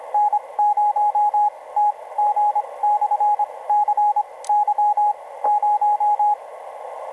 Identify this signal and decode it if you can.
Morse code, heard as a keyed tone, about 26 wpm, spaced as for 19 wpm, 830 Hz, with radio noise. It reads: N0TG/CY0
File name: n0tg-cy0 ve1mc etc 30m cw 07mar2011 2034